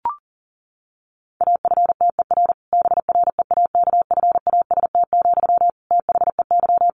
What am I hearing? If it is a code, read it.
Morse code, heard as a keyed tone, about 40 wpm, 700 Hz. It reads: AFTER BREAKFAST, THEY
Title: Sentences from Top 1500 Words 40wpm